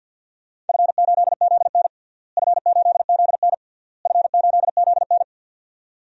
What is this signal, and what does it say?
Morse code, heard as a keyed tone, about 53 wpm, 700 Hz. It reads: F8ZN F8ZN F8ZN
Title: DXCC-100 Call Signs - 4 Characters - Repeated 3x - 50wpm